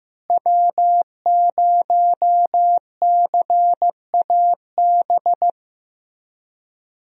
Morse code, heard as a keyed tone, about 15 wpm, 700 Hz. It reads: W0CAB
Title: USA Call Signs - 5 Characters - Repeated 3x - 15wpm